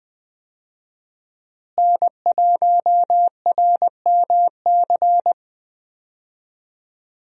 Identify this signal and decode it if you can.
Morse code, heard as a keyed tone, about 20 wpm, 700 Hz. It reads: N1RMC
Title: USA Call Signs - 5 Characters - Repeated 3x - 20wpm